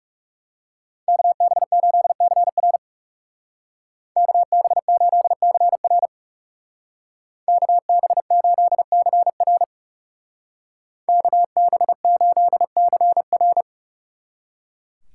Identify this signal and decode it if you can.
Morse code, heard as a keyed tone, about 37 wpm, 700 Hz. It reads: KB8CR KB8CR KB8CR KB8CR
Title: Speed Racing - USA 2x2 Call Signs 30wpm